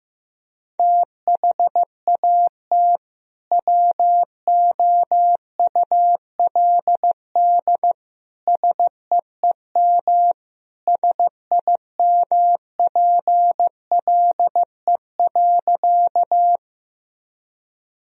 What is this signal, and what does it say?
Morse code, heard as a keyed tone, about 15 wpm, 700 Hz. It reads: THAT WOULD SEEM SIMPLE.
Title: Sentences from Top 1000 Words - Review Part 1 - 15wpm